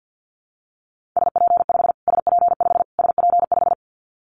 Morse code, heard as a keyed tone, about 48 wpm, 700 Hz. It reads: SP5 SP5 SP5
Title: Callsign Prefixes for Top 25 DXCC - Repeated 3x - 50wpm